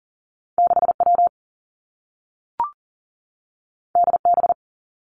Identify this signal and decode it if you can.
Morse code, heard as a keyed tone, about 39 wpm, 700 Hz. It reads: 6 W D B
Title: Two Challenging Letter-Number Combinations 40wpm